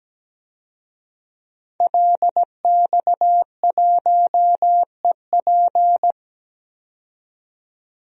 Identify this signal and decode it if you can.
Morse code, heard as a keyed tone, about 17 wpm, 700 Hz. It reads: LX1EP
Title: DXCC-100 Call Signs - 5 Characters - 17wpm